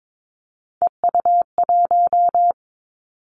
Morse code, heard as a keyed tone, about 22 wpm, 700 Hz. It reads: EU1
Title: Callsign Prefixes for Top 50 DXCC - 22wpm